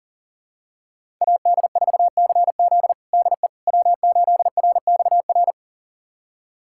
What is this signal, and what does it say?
Morse code, heard as a keyed tone, about 40 wpm, 700 Hz. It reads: AD4CZ DE W8RXR
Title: USA Callsign DE Callsign - 5 Characters - 40wpm